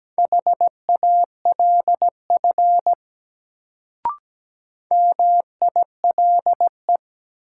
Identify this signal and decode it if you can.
Morse code, heard as a keyed tone, about 17 wpm, 700 Hz. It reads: HALF MILE
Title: Sentences from Top 400 Words - New Word Drill - 17wpm